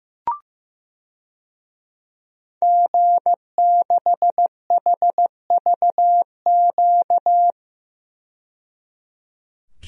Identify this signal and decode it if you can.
Morse code, heard as a keyed tone, about 15 wpm, 700 Hz. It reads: G6HVQ
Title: DXCC-100 Call Signs - 5 Characters - 15wpm